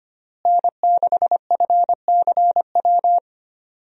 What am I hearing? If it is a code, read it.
Morse code, heard as a keyed tone, about 25 wpm, 700 Hz. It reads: N6FCW
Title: USA Call Signs - 5 Characters - Repeated 3x - 25wpm